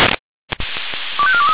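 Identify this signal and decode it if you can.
Five-tone selective call (ZVEI), heard as quick runs of five tones, tones of about 70 ms. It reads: 25423
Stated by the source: ZVEI five-tone selective call received on an SDR.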